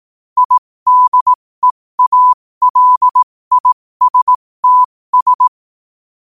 Morse code, heard as a keyed tone, about 18 wpm, spaced as for 15 wpm, 1000 Hz. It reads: IDEALISTS